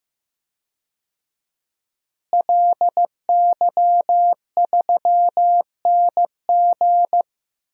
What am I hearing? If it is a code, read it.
Morse code, heard as a keyed tone, about 15 wpm, 700 Hz. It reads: LY3NG
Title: DXCC-100 Call Signs - 5 Characters - Repeated 2x - 15wpm